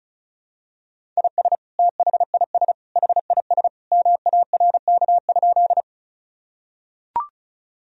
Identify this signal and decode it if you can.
Morse code, heard as a keyed tone, about 36 wpm, 700 Hz. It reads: IS THIS HIS MARK?
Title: Sentences from Top 1000 Words - Review Part 1 - 35wpm